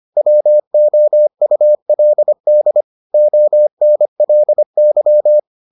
Morse code, heard as a keyed tone, about 25 wpm, 600 Hz. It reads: WOULD ONLY